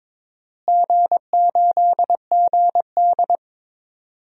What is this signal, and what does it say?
Morse code, heard as a keyed tone, about 22 wpm, 700 Hz. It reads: G8GD
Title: DXCC-100 Call Signs - 4 Characters - Repeated 3x - 22wpm